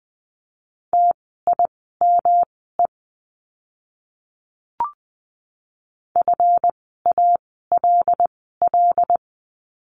Morse code, heard as a keyed tone, about 20 wpm, spaced as for 15 wpm, 700 Hz. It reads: TIME FALL
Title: 139 Lesson 23f - Letter f - Words 20wpm